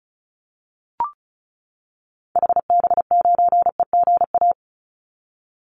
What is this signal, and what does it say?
Morse code, heard as a keyed tone, about 34 wpm, 700 Hz. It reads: HB9EGA
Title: Call Signs - 35wpm